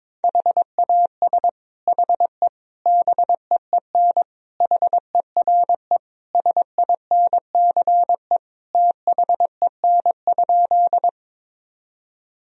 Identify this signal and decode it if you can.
Morse code, heard as a keyed tone, about 22 wpm, 700 Hz. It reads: HAS HE BEEN HERE SINCE THEN?
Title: Sentences from Top 1000 Words - Review Part 1 - 22wpm